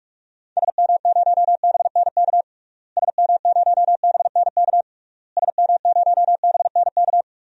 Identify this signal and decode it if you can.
Morse code, heard as a keyed tone, about 46 wpm, 700 Hz. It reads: SM0BNK SM0BNK SM0BNK
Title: DXCC-100 Call Signs - 6 Characters - Repeated 3x - 45wpm